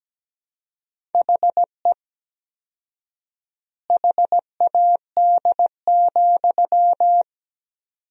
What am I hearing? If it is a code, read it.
Morse code, heard as a keyed tone, about 17 wpm, 700 Hz. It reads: HE HAD,
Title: 4x Word-Spacing - Sentences from Top 500 Words - 17wpm